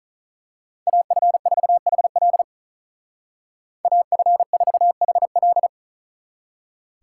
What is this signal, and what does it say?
Morse code, heard as a keyed tone, about 39 wpm, 700 Hz. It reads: AF4HL AF4HL
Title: Speed Racing - USA 2x2 Call Signs 35wpm